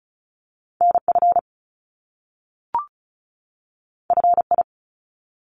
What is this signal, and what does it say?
Morse code, heard as a keyed tone, about 35 wpm, 700 Hz. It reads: NF FI
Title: Top Two Letter Combinations in Words 35wpm